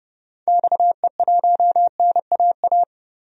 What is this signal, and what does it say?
Morse code, heard as a keyed tone, about 30 wpm, 700 Hz. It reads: XE1NAA
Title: DXCC-100 Call Signs - 6 Characters - 30wpm